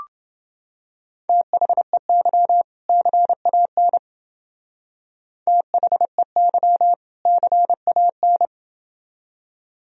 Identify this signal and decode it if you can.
Morse code, heard as a keyed tone, about 28 wpm, 700 Hz. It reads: THEY CAN THEY CAN